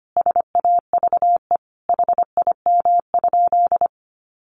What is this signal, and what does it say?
Morse code, heard as a keyed tone, about 25 wpm, 700 Hz. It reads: SAVE HIM?
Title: Sentences from Top 1100 Words 25wpm